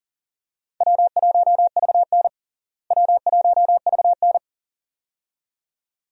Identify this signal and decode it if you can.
Morse code, heard as a keyed tone, about 37 wpm, 700 Hz. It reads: W1VN W1VN
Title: USA Call Signs - 4 Characters - Repeated 2x - 40wpm